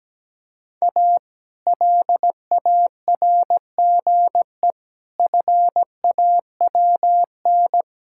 Morse code, heard as a keyed tone, about 17 wpm, 700 Hz. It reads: A LARGE FAWN
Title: Pangram Challenge v2 - 17wpm